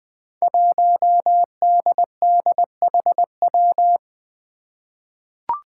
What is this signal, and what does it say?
Morse code, heard as a keyed tone, about 20 wpm, 700 Hz. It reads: 1DDHW
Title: Five Challenging Letter-Number Combinations 20wpm